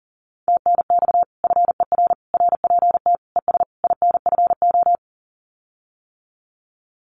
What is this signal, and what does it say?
Morse code, heard as a keyed tone, about 38 wpm, 700 Hz. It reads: TNX FER RPT ES INFO
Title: Common QSO Phrases Encore 40wpm